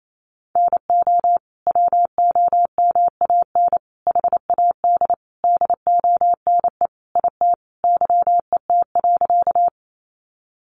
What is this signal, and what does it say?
Morse code, heard as a keyed tone, about 28 wpm, 700 Hz. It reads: NO WOMAN HAD DONE IT YET.
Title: Sentences from Top 1100 Words 28wpm